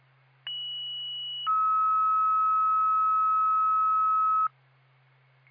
Two-tone page, heard as a one-second tone, then a three-second tone, about 2950 then 1290 Hz. It is heard from 0.5 to 4.5 s.